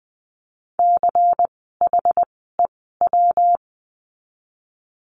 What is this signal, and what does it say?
Morse code, heard as a keyed tone, about 20 wpm, 700 Hz. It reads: C H E W
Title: 115 Lesson 19f - Letter w - Words 20wpm